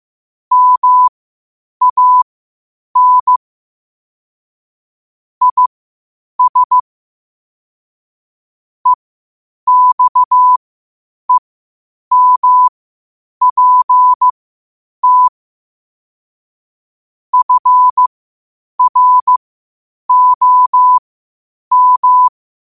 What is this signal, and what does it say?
Morse code, heard as a keyed tone, about 15 wpm, spaced as for 8 wpm, 1000 Hz. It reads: MAN IS EXEMPT FROM
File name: qotdfiles / 8wpm